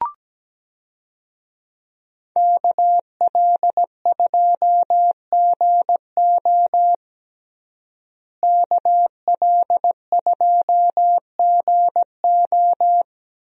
Morse code, heard as a keyed tone, about 17 wpm, 700 Hz. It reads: KL2GO KL2GO
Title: USA Call Signs - 5 Characters - Repeated 2x - 17wpm